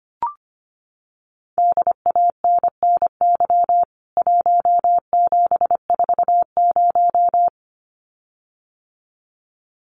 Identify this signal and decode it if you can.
Morse code, heard as a keyed tone, about 25 wpm, 700 Hz. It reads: DANNY 1740
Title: CWT Exchange 25wpm